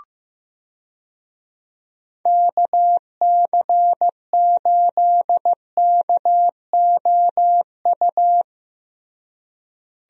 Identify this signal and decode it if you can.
Morse code, heard as a keyed tone, about 15 wpm, 700 Hz. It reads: KC8KOU
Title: USA Call Signs - 6 Characters - Repeated 3x - 15wpm